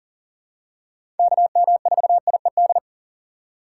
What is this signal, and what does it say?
Morse code, heard as a keyed tone, about 37 wpm, 700 Hz. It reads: KM4IED
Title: USA Call Signs - 6 Characters - 4x Character Spacing - 40wpm